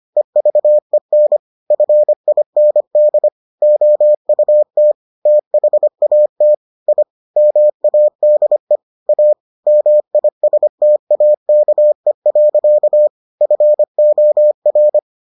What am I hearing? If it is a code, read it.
Morse code, heard as a keyed tone, about 25 wpm, 600 Hz. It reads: EVEN FIND OUT THAT I MADE A MISTAKE. FOR